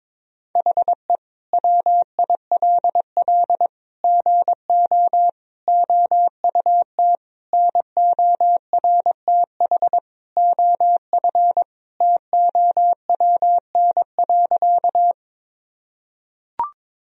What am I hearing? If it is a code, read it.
Morse code, heard as a keyed tone, about 22 wpm, 700 Hz. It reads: HE WILL GO OUT NORTH OF TOWN.
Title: Sentences from Top 1000 Words - Review Part 1 - 22wpm